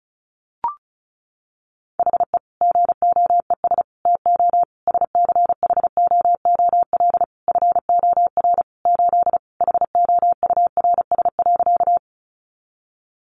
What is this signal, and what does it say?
Morse code, heard as a keyed tone, about 34 wpm, 700 Hz. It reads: HE GOES TO SCHOOL FOR 8 HOURS.
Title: Additional Easy Phrases 35wpm